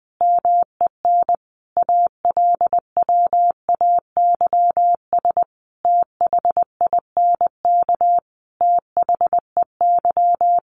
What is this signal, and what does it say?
Morse code, heard as a keyed tone, about 20 wpm, 700 Hz. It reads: MEN ALWAYS THINK THEY
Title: Sentences from Top 600 Words 20wpm